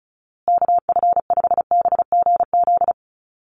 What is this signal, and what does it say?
Morse code, heard as a keyed tone, about 34 wpm, 700 Hz. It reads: KF5BGZ
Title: USA Call Signs Repeated 3x 35wpm